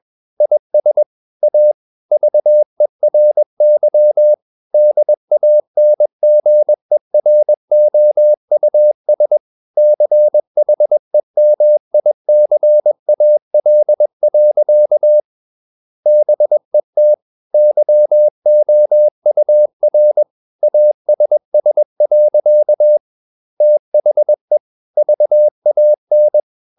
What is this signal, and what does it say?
Morse code, heard as a keyed tone, about 21 wpm, 600 Hz. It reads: IS A VERY DANGEROUS CHEMICAL. BET YOUR ASS. THE VAN
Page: CW med Gnister